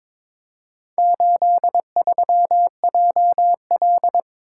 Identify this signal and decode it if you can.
Morse code, heard as a keyed tone, about 22 wpm, 700 Hz. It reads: 83JL
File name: practice / 22wpm / Four Letter-Number Combinations - 22wpm